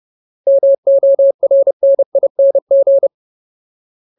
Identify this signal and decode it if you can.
Morse code, heard as a keyed tone, about 30 wpm, 550 Hz. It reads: MORNING
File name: Ultimate Word List / 30WPM Ultimate